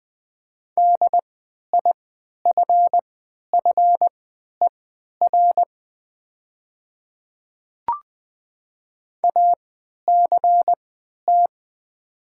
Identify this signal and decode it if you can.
Morse code, heard as a keyed tone, about 20 wpm, spaced as for 11 wpm, 700 Hz. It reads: DIFFER ACT
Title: 3x Character Spacing - Top 200 Words - 20wpm